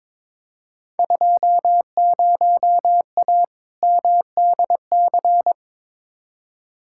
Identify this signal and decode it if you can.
Morse code, heard as a keyed tone, about 22 wpm, 700 Hz. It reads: 20A MDC
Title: Field Day Exchanges Encore - 22wpm